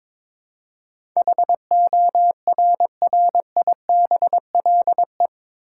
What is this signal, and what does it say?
Morse code, heard as a keyed tone, about 21 wpm, 700 Hz. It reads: HORRIBLE
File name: practice / 22wpm / Eight Letter Words - 22wpm